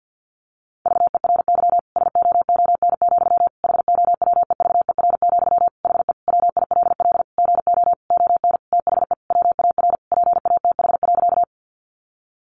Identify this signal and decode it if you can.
Morse code, heard as a keyed tone, about 48 wpm, 700 Hz. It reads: VERY SOON, HOWEVER, HE WILL GO ON THE WAR PATH.
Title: Sentences from Top 1400 Words 50wpm